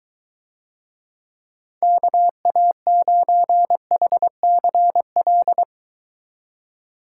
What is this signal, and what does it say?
Morse code, heard as a keyed tone, about 21 wpm, 700 Hz. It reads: KA9HCL